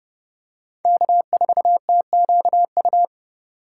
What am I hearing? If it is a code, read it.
Morse code, heard as a keyed tone, about 26 wpm, 700 Hz. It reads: K4TQU